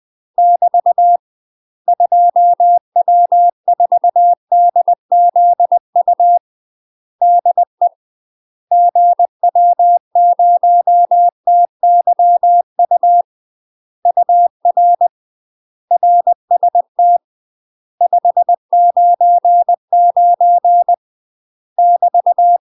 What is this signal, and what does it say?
Morse code, heard as a keyed tone, about 20 wpm, 700 Hz. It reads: <BT> 2W4DZU DE GW0TYU UR RST 599 <BT>
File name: UK MASTER QSOs TRAINING / 20WPM / UK MASTER QSOs 0004